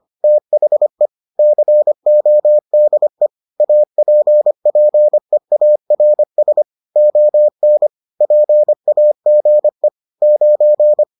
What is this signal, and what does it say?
Morse code, heard as a keyed tone, about 25 wpm, 600 Hz. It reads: THE CODE APPEARS ON PAGE 9